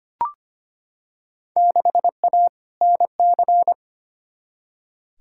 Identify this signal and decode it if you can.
Morse code, heard as a keyed tone, about 25 wpm, 700 Hz. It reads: 6A NC
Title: Field Day Exchanges Encore - 25wpm